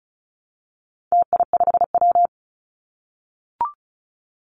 Morse code, heard as a keyed tone, about 34 wpm, 700 Hz. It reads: TI5W
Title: Call Signs - 35wpm